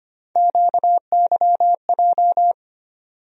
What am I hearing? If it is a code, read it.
Morse code, heard as a keyed tone, about 25 wpm, 700 Hz. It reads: QYJ
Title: USA Call Sign Postfixes - 3 Letters 25wpm